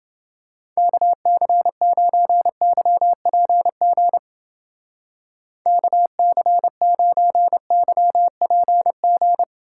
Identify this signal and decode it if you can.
Morse code, heard as a keyed tone, about 28 wpm, 700 Hz. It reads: KC9YPG KC9YPG